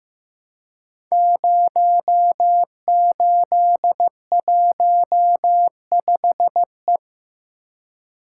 Morse code, heard as a keyed tone, about 15 wpm, 700 Hz. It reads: 0815E
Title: Code Groups - 5 Letter:Numbers - 15wpm